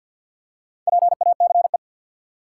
Morse code, heard as a keyed tone, about 53 wpm, 700 Hz. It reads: PACE
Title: Four Letter Words - 50wpm